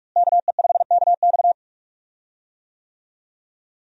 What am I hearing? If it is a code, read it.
Morse code, heard as a keyed tone, about 46 wpm, 700 Hz. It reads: KE5KX
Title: USA Call Signs - 5 Characters - Repeated 3x - 45wpm